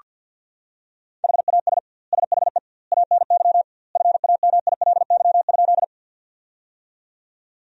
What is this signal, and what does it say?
Morse code, heard as a keyed tone, about 46 wpm, 700 Hz. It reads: HAS SHE ANY FAMILY?
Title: 2x Word-Spacing - Sentences from Top 400 Words - 50wpm